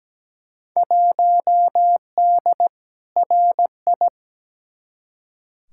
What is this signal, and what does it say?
Morse code, heard as a keyed tone, about 17 wpm, 700 Hz. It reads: 1D RI
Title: Field Day Exchanges - 17wpm